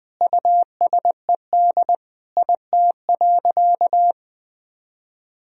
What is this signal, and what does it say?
Morse code, heard as a keyed tone, about 20 wpm, 700 Hz. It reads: USED IT.
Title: Sentences from Top 1000 Words - Review Part 2 - 20wpm